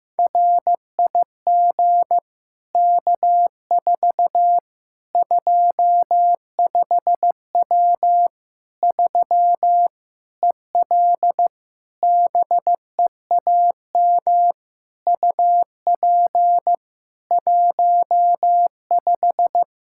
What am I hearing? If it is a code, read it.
Morse code, heard as a keyed tone, about 15 wpm, 700 Hz. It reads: RIG K4 25W 3 EL BEAM UP 15
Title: Practice QSO - Exchange 2 - Info - 15wpm